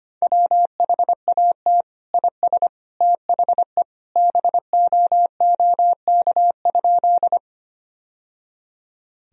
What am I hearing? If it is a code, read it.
Morse code, heard as a keyed tone, about 25 wpm, 700 Hz. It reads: WHAT IS THE BOOK?
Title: Sentences from Top 1000 Words - Review Part 1 - 25wpm